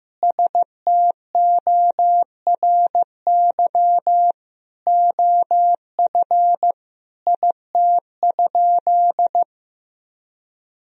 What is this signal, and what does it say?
Morse code, heard as a keyed tone, about 15 wpm, 700 Hz. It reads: STORY OF IT?
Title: Sentences from Top 1000 Words - Review Part 1 - 15wpm